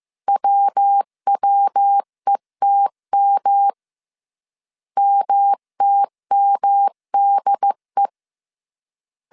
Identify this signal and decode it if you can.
Morse code, heard as a keyed tone, about 15 wpm, 800 Hz. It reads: WWETM MTMDE
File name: CW / DET / 1 3 1500C S A 14wpm 2447